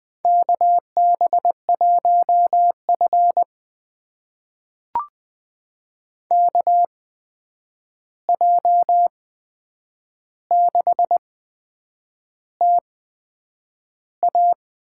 Morse code, heard as a keyed tone, about 21 wpm, spaced as for 8 wpm, 700 Hz. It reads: KB1F KJ6TA
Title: USA Call Signs - 8x Character Spacing - 20wpm